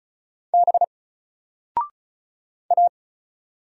Morse code, heard as a keyed tone, about 35 wpm, 700 Hz. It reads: B A
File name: Single-Letter-Number-35wpm